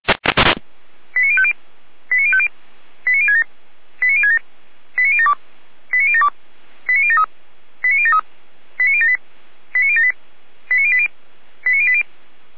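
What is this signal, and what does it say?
Five-tone selective call (ZVEI), heard as quick runs of five tones, tones of about 70 ms. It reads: 89050 89050 89067 89067 89072 89072 89073 89073 89078 89078 89080 89080
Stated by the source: Five-tone ZVEI selective call received by SDR.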